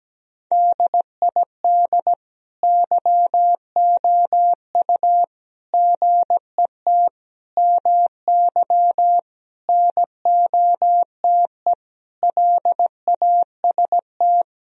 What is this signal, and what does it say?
Morse code, heard as a keyed tone, about 17 wpm, 700 Hz. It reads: DID YOU GET MY NOTE LAST
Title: Sentences from Top 1000 Words - Review Part 2 - 17wpm